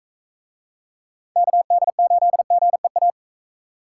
Morse code, heard as a keyed tone, about 43 wpm, 700 Hz. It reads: KD8GEA